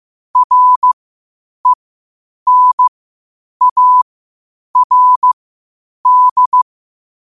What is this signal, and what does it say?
Morse code, heard as a keyed tone, about 15 wpm, spaced as for 8 wpm, 1000 Hz. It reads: RENARD
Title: 8 WPM morse code quotes for Thu, 29 Feb 2024